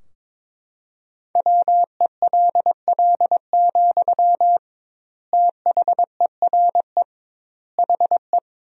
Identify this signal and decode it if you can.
Morse code, heard as a keyed tone, about 22 wpm, 700 Hz. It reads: WELL, THERE HE
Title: 2x Word-Spacing - Sentences from Top 500 Words - 22wpm